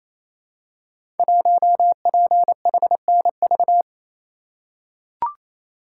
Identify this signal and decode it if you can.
Morse code, heard as a keyed tone, about 29 wpm, 700 Hz. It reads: 1PHNV
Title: Code Groups - 5 Letter:Numbers - 28wpm